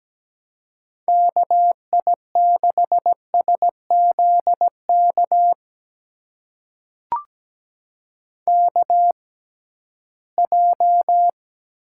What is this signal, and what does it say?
Morse code, heard as a keyed tone, about 17 wpm, spaced as for 13 wpm, 700 Hz. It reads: KI6SZK KJ
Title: USA Call Signs - 6 Characters - 6x Character Spacing - 17wpm